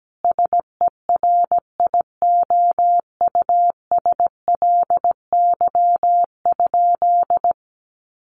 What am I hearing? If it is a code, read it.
Morse code, heard as a keyed tone, about 17 wpm, 700 Hz. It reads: SERIOUSLY?
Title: Sentences from Top 1200 Words 17wpm